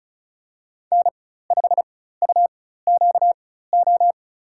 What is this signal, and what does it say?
Morse code, heard as a keyed tone, about 36 wpm, spaced as for 16 wpm, 700 Hz. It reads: N5UQO
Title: USA Call Signs - 4x Character Spacing - 35wpm